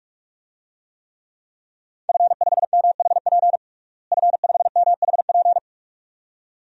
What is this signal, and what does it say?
Morse code, heard as a keyed tone, about 46 wpm, 700 Hz. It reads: F5MHP F5MHP
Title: DXCC-100 Call Signs - 5 Characters - Repeated 2x - 45wpm